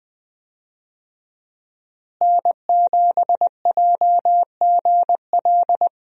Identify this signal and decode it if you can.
Morse code, heard as a keyed tone, about 20 wpm, 700 Hz. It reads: N7JGL
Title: USA Call Signs - 5 Characters - Repeated 3x - 20wpm